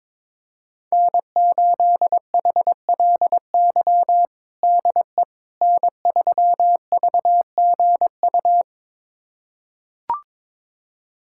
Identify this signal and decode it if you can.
Morse code, heard as a keyed tone, about 22 wpm, 700 Hz. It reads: N8HLY DE N3VGU
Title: USA Callsign DE Callsign - 5 Characters - 22wpm